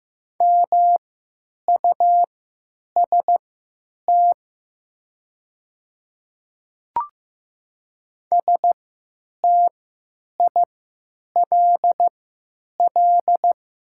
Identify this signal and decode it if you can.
Morse code, heard as a keyed tone, about 15 wpm, spaced as for 9 wpm, 700 Hz. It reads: MUST STILL
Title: 3x Character Spacing - Top 300 Words - 15wpm